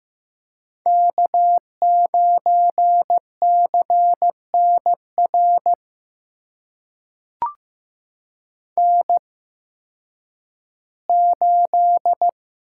Morse code, heard as a keyed tone, about 15 wpm, spaced as for 10 wpm, 700 Hz. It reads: K9CNR N8
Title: USA Call Signs - 5 Characters - 8x Character Spacing - 15wpm